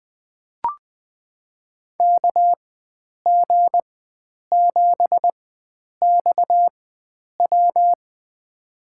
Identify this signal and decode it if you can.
Morse code, heard as a keyed tone, about 20 wpm, spaced as for 9 wpm, 700 Hz. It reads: KG7XW
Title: USA Call Signs - 5 Characters - 4x Character Spacing - 20wpm